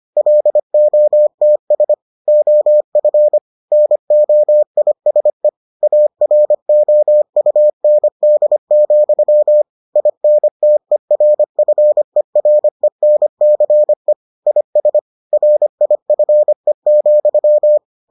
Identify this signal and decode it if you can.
Morse code, heard as a keyed tone, about 25 wpm, 600 Hz. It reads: LOTS OF NOISE AROUND, INTERFERENCE IS RIFE,